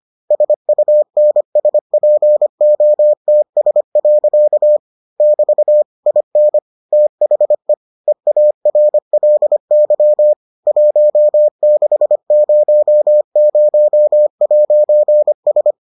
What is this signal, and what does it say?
Morse code, heard as a keyed tone, about 25 wpm, 600 Hz. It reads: SUNSPOTS. <BT> IN THE EARLY 1600'S